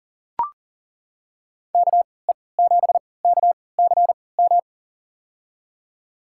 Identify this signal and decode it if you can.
Morse code, heard as a keyed tone, about 40 wpm, spaced as for 23 wpm, 700 Hz. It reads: KE7KCM
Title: USA Call Signs - 6 Characters - 3x Character Spacing - 40wpm